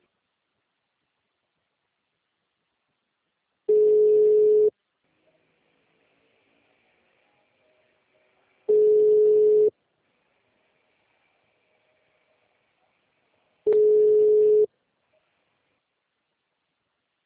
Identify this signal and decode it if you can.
Ringback tone (Europe, 425 Hz).